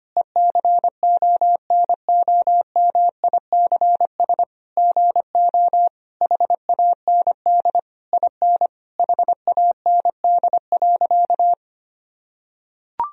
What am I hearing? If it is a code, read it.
Morse code, heard as a keyed tone, about 25 wpm, 700 Hz. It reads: ECONOMICS GO HAND IN HAND.
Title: Sentences from Top 1000 Words - Review Part 3 - 25wpm